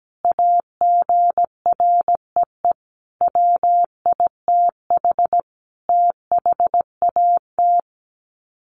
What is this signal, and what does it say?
Morse code, heard as a keyed tone, about 17 wpm, 700 Hz. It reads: AGREE WITH THAT
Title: Sets of 5 Words Encore 17wpm